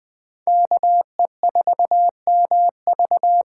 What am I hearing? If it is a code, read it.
Morse code, heard as a keyed tone, about 20 wpm, 700 Hz. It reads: KE4MV
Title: Speed Racing - USA 2x2 Call Signs 17wpm